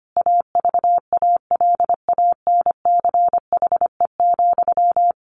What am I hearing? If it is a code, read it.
Morse code, heard as a keyed tone, about 25 wpm, 700 Hz. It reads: AVALANCHE,
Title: A Sound Of Thunder 25wpm